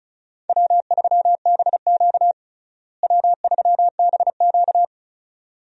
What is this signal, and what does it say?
Morse code, heard as a keyed tone, about 36 wpm, 700 Hz. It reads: W3BQ W3BQ
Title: USA Call Signs - 4 Characters - Repeated 2x - 35wpm